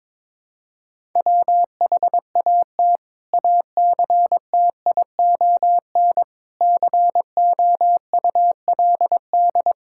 Morse code, heard as a keyed tone, about 22 wpm, 700 Hz. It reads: WHAT ACTION COULD
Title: Sentences from Top 1000 Words - Review Part 6 - 22wpm